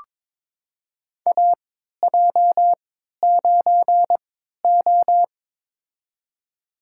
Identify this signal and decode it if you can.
Morse code, heard as a keyed tone, about 22 wpm, spaced as for 13 wpm, 700 Hz. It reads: AJ9O
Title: USA Call Signs - 4 Characters - 3x Character Spacing - 22wpm